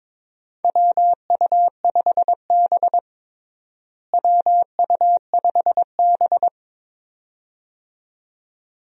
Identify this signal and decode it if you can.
Morse code, heard as a keyed tone, about 22 wpm, 700 Hz. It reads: WU5B WU5B
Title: USA Call Signs - 4 Characters - Repeated 2x - 22wpm